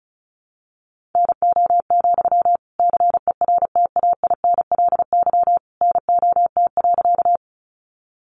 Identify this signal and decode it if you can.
Morse code, heard as a keyed tone, about 34 wpm, 700 Hz. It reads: NO, CERTAINLY NOT.
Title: Sentences from Top 900 Words 35wpm